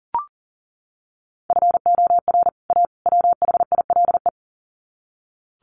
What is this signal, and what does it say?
Morse code, heard as a keyed tone, about 40 wpm, 700 Hz. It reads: FOR A WHILE
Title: Sets of 3 Words 40wpm